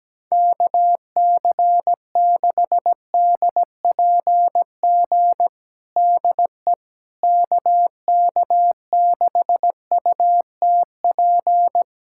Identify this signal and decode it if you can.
Morse code, heard as a keyed tone, about 17 wpm, 700 Hz. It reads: KC6DPG DE KK6UTP
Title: USA Callsign DE Callsign - 6 Characters - 17wpm